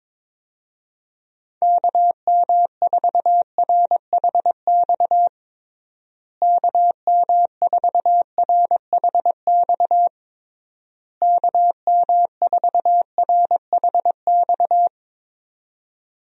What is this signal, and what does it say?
Morse code, heard as a keyed tone, about 22 wpm, 700 Hz. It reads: KM4RHX KM4RHX KM4RHX
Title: USA Call Signs - 6 Characters - Repeated 3x - 22wpm